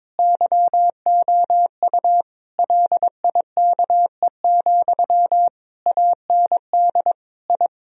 Morse code, heard as a keyed tone, about 22 wpm, 700 Hz. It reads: YOU LIKE, AND I
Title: Sentences from Top 1000 Words - Review Part 2 - 22wpm